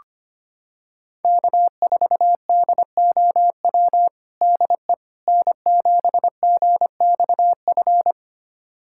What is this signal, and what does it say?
Morse code, heard as a keyed tone, about 25 wpm, 700 Hz. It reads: K4DOW DE N7GXF
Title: USA Callsign DE Callsign - 5 Characters - 25wpm